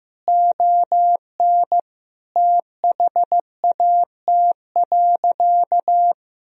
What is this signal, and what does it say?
Morse code, heard as a keyed tone, about 15 wpm, 700 Hz. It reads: ON THAT.
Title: Sentences from Top 1000 Words - Review Part 1 - 15wpm